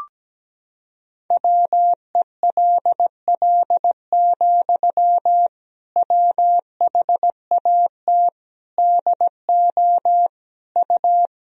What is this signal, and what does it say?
Morse code, heard as a keyed tone, about 17 wpm, 700 Hz. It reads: WELL, WHAT DO U
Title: Sentences from Top 200 Words and Abbreviations 17wpm